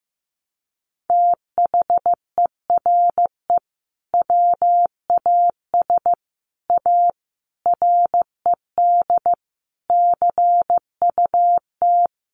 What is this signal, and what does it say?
Morse code, heard as a keyed tone, about 15 wpm, 700 Hz. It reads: THERE WAS A RED CUT
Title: Sentences from Top 700 Words 15wpm